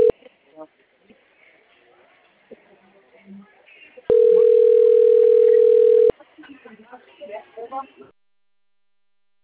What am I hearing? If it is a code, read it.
Ringback tone (North America), 440 + 480 Hz, with speech.
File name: rg-1600-unknown- - - .68167